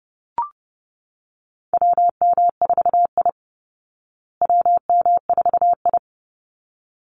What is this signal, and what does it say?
Morse code, heard as a keyed tone, about 30 wpm, 700 Hz. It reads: WM4I WM4I
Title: Call Signs Repeated 3x - 30wpm